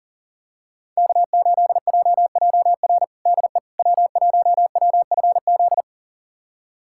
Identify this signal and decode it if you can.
Morse code, heard as a keyed tone, about 40 wpm, 700 Hz. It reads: K8JJR DE W1WFZ
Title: USA Callsign DE Callsign - 5 Characters - 40wpm